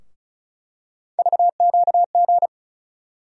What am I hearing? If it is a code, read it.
Morse code, heard as a keyed tone, about 36 wpm, 700 Hz. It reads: VQG
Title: USA Call Sign Postfixes - 3 Letters 35wpm